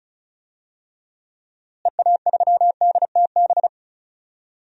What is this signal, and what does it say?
Morse code, heard as a keyed tone, about 34 wpm, 700 Hz. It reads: EA3DTB
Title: DXCC-100 Call Signs - 6 Characters - Repeated 3x - 35wpm